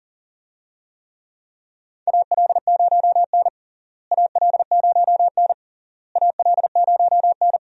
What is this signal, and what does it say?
Morse code, heard as a keyed tone, about 40 wpm, 700 Hz. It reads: AL0N AL0N AL0N
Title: USA Call Signs - 4 Characters - Repeated 3x - 40wpm